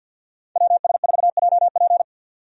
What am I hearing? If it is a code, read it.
Morse code, heard as a keyed tone, about 50 wpm, 700 Hz. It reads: WS4JP
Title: USA Call Signs - 5 Characters - 50wpm